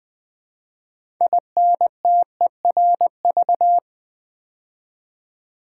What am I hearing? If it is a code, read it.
Morse code, heard as a keyed tone, about 20 wpm, 700 Hz. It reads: INTERV
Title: Build A Word - 2 Letters At A Time - 8-Letter Words - 20wpm